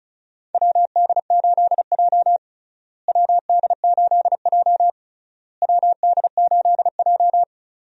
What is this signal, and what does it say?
Morse code, heard as a keyed tone, about 34 wpm, 700 Hz. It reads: WD8J WD8J WD8J
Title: USA Call Signs - 4 Characters - Repeated 3x - 35wpm